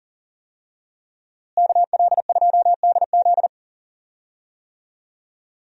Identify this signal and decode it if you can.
Morse code, heard as a keyed tone, about 40 wpm, 700 Hz. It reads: KL2DZ
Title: USA Call Signs - 5 Characters - Repeated 2x - 40wpm